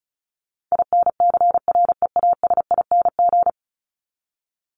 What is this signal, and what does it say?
Morse code, heard as a keyed tone, about 34 wpm, 700 Hz. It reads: INCREASING
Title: ING Suffix - Top 100 Words with Base Words 35wpm